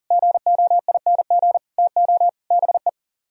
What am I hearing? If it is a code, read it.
Morse code, heard as a keyed tone, about 37 wpm, 700 Hz. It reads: GOING TO BE
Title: Speed-Racing - Sets of 4 Words 30wpm